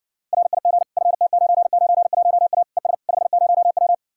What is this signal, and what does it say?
Morse code, heard as a keyed tone, about 60 wpm, 700 Hz. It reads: RIG FT991A ES 50W